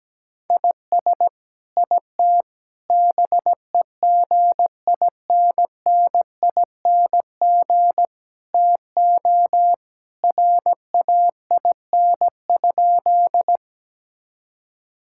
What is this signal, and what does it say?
Morse code, heard as a keyed tone, about 17 wpm, 700 Hz. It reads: IS IT BEGINNING TO RAIN?
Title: Sentences from Top 1000 Words - Review Part 5 - 17wpm